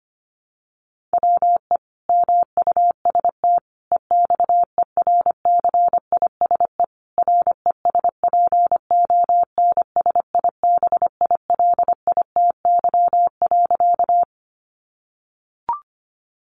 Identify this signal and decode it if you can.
Morse code, heard as a keyed tone, about 25 wpm, 700 Hz. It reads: WE MUST EXERCISE RESPONSIBILITY.
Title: Sentences from Top 1200 Words 25wpm